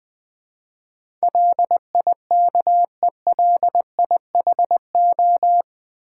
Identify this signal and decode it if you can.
Morse code, heard as a keyed tone, about 20 wpm, 700 Hz. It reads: LIKELIHO
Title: Build A Word - 4 Letters At A Time - 10-Letter Words - 20wpm